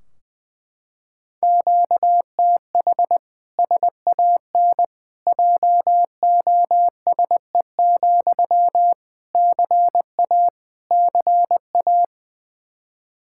Morse code, heard as a keyed tone, about 21 wpm, 700 Hz. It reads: QTH SAN JOSE, CA CA
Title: Practice QSO - Exchange 1 - Beginning - 20wpm